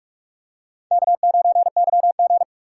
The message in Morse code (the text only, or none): K9YG